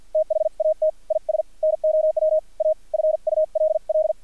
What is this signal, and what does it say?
Morse code, heard as a keyed tone, about 50 wpm, spaced as for 43 wpm, 630 Hz. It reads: THAT IS NOW A FULL
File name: archived / CRSnet